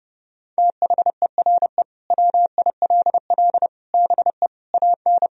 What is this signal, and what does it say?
Morse code, heard as a keyed tone, about 32 wpm, 700 Hz. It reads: THERE WILL BE AN
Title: Sentences from Top 1000 Words - Review Part 4 - 30wpm